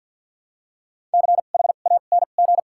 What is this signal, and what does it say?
Morse code, heard as a keyed tone, about 51 wpm, spaced as for 36 wpm, 700 Hz. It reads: CHANG
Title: 2x Character Spacing - Top 500 Words - 50wpm